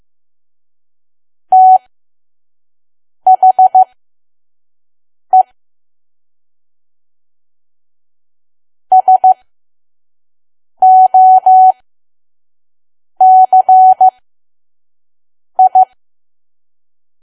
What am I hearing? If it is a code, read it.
Morse code, heard as a keyed tone, about 15 wpm, spaced as for 5 wpm, 750 Hz. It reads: THE SOCI
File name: Archive / 5 WPM files / PM